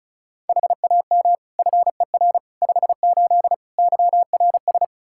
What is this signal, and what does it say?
Morse code, heard as a keyed tone, about 36 wpm, 700 Hz. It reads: HAM FER 58 YRS
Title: K1USN SST QSO Practice 35wpm